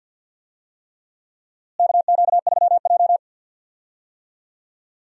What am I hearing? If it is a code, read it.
Morse code, heard as a keyed tone, about 52 wpm, 700 Hz. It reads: KQ3J
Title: USA Call Signs - 4 Characters - Repeated 3x - 50wpm